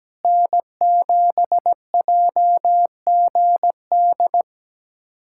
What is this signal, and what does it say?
Morse code, heard as a keyed tone, about 17 wpm, 700 Hz. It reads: N7JGD
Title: USA Callsign DE Callsign - 5 Characters - 17wpm